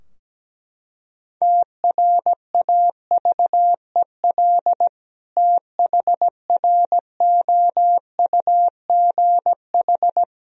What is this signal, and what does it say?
Morse code, heard as a keyed tone, about 17 wpm, 700 Hz. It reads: TRAVEL THROUGH
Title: Sentences from Top 1000 Words - Review Part 2 - 17wpm